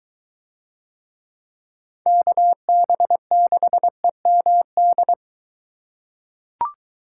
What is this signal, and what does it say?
Morse code, heard as a keyed tone, about 22 wpm, 700 Hz. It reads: KB6EMD